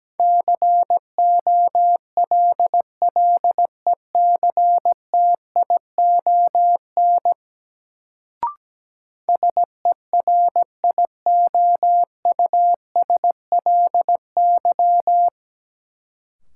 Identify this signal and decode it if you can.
Morse code, heard as a keyed tone, about 17 wpm, 700 Hz. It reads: COLLECTION SERIOUSLY
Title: Sentences from Top 1200 Words - New Word Drill Alt - 17wpm